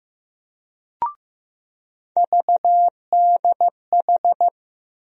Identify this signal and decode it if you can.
Morse code, heard as a keyed tone, about 16 wpm, 700 Hz. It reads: VDH
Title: USA Call Sign Postfixes - 3 Letters 15wpm